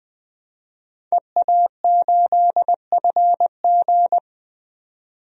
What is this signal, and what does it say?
Morse code, heard as a keyed tone, about 20 wpm, 700 Hz. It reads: EA8FG
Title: DXCC-100 Call Signs - 5 Characters - 20wpm